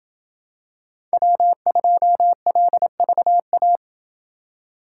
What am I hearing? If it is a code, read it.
Morse code, heard as a keyed tone, about 25 wpm, 700 Hz. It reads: W2LVA
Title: Speed Racing - USA 1x3 Call Signs 20wpm